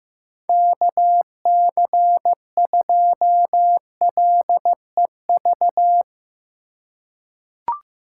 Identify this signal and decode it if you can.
Morse code, heard as a keyed tone, about 15 wpm, 700 Hz. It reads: KC2LEV
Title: USA Call Signs - 6 Characters - 2x Character Spacing - 15wpm